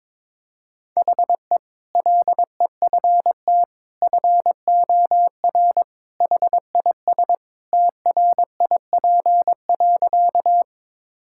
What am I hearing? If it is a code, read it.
Morse code, heard as a keyed tone, about 22 wpm, 700 Hz. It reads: HE LEFT FOR HIS TRIP.
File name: practice / 22wpm / Sentences with 2-Word Phrases - 22wpm